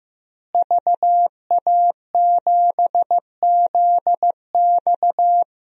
Morse code, heard as a keyed tone, about 15 wpm, 700 Hz. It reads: VA7ZX
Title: DXCC-100 Call Signs - 5 Characters - Repeated 2x - 15wpm